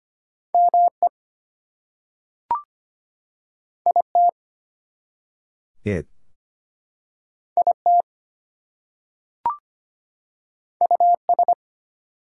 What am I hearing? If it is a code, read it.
Morse code, heard as a keyed tone, about 25 wpm, 700 Hz. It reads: ME IT IT US
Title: Two Letter Words - 25wpm